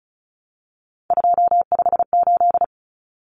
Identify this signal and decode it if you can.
Morse code, heard as a keyed tone, about 34 wpm, 700 Hz. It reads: 258
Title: Numbers - 3-Digit 35wpm